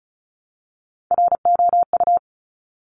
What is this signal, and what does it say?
Morse code, heard as a keyed tone, about 36 wpm, 700 Hz. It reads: ROU
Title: Top Three Letter Combinations in Words 35wpm